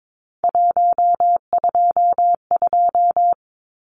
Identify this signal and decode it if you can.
Morse code, heard as a keyed tone, about 22 wpm, 700 Hz. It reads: 122
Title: Numbers - 3-Digit 22wpm